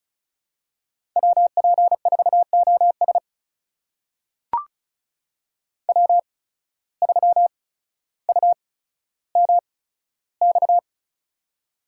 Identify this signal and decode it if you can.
Morse code, heard as a keyed tone, about 36 wpm, spaced as for 15 wpm, 700 Hz. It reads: WP4OS W3UMX
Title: USA Call Signs - 5 Characters - 8x Character Spacing - 35wpm